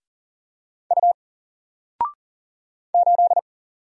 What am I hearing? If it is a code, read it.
Morse code, heard as a keyed tone, about 40 wpm, 700 Hz. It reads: U 8
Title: Single Character (Letter:Number:Punctuation:Prosign) 40wpm